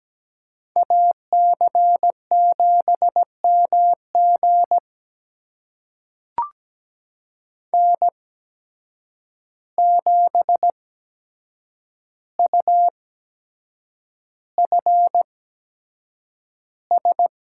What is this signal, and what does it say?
Morse code, heard as a keyed tone, about 17 wpm, spaced as for 7 wpm, 700 Hz. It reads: AC7MG N7UFS
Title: USA Call Signs - 5 Characters - 8x Character Spacing - 17wpm